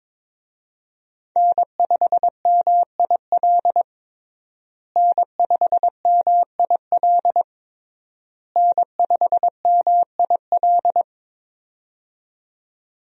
Morse code, heard as a keyed tone, about 21 wpm, 700 Hz. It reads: N5MIL N5MIL N5MIL
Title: USA Call Signs - 5 Characters - Repeated 3x - 22wpm